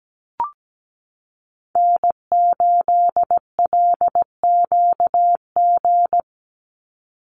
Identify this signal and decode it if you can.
Morse code, heard as a keyed tone, about 17 wpm, 700 Hz. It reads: N8LQG
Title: USA Call Signs 17wpm